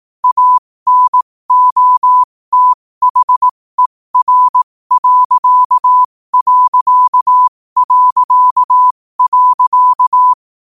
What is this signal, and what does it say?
Morse code, heard as a keyed tone, about 18 wpm, spaced as for 15 wpm, 1000 Hz. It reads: ANOTHER....
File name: qotdfiles / 15wpm